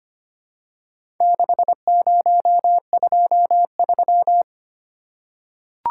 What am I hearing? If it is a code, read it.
Morse code, heard as a keyed tone, about 25 wpm, 700 Hz. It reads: 6023
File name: practice / 25wpm / Four Letter-Number Combinations - 25wpm